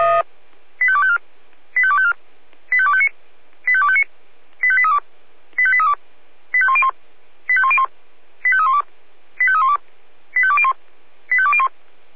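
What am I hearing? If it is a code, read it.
Five-tone selective call (ZVEI), heard as quick runs of five tones, tones of about 70 ms. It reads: 85254 85254 85259 85259 85832 85832 85101 85101 85312 85312 85201 85201
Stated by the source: ZVEI five-tone selective call received on SDR.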